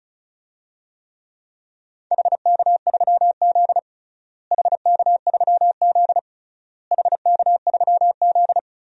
Morse code, heard as a keyed tone, about 36 wpm, 700 Hz. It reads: HK3Z HK3Z HK3Z
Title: DXCC-100 Call Signs - 4 Characters - Repeated 3x - 35wpm